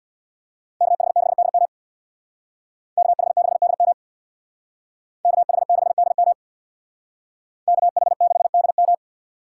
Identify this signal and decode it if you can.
Morse code, heard as a keyed tone, about 63 wpm, 690 Hz. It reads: KH6DM KH6DM KH6DM KH6DM